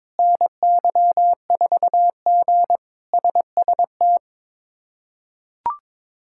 Morse code, heard as a keyed tone, about 22 wpm, 700 Hz. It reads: NY4G SST
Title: K1USN SST QSO Practice - 22wpm